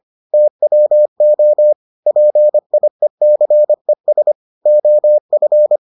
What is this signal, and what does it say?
Morse code, heard as a keyed tone, about 25 wpm, 600 Hz. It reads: TWO PIECES OF